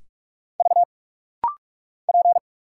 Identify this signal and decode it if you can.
Morse code, heard as a keyed tone, about 46 wpm, 700 Hz. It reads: V P
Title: Single Letters - Rapid-Fire 45wpm